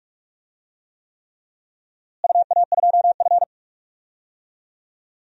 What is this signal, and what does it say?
Morse code, heard as a keyed tone, about 41 wpm, 700 Hz. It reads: UA2F
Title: DXCC-100 Call Signs - 4 Characters - Repeated 3x - 45wpm